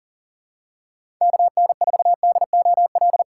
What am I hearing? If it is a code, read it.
Morse code, heard as a keyed tone, about 40 wpm, 700 Hz. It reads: KN4DOL
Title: USA Call Signs - 6 Characters - 3x Character Spacing - 40wpm